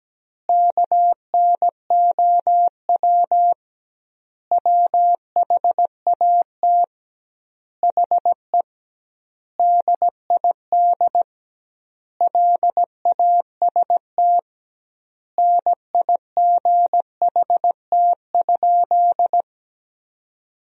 Morse code, heard as a keyed tone, about 17 wpm, 700 Hz. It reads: KNOW WHAT HE DID LAST NIGHT?
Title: 2x Word-Spacing - Sentences from Top 300 Words - 17wpm